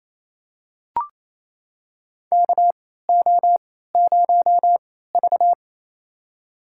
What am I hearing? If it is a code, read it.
Morse code, heard as a keyed tone, about 28 wpm, spaced as for 16 wpm, 700 Hz. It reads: KO0V
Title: USA Call Signs - 3x Character Spacing - 28wpm